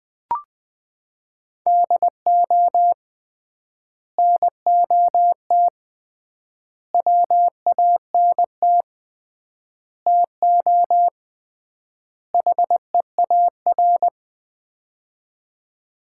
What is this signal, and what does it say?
Morse code, heard as a keyed tone, about 20 wpm, 700 Hz. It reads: DO NOT WANT TO HEAR
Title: 3x Word-Spacing - Sets of 5 Words - 20wpm